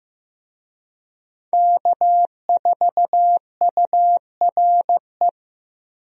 Morse code, heard as a keyed tone, about 16 wpm, 700 Hz. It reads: K4URE
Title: USA Call Signs - 5 Characters - Repeated 3x - 15wpm